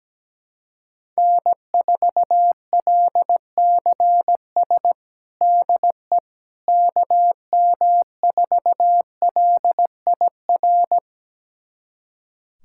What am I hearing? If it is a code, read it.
Morse code, heard as a keyed tone, about 17 wpm, 700 Hz. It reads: N4LCS DE KM4LIR
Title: USA Callsign DE Callsign 17wpm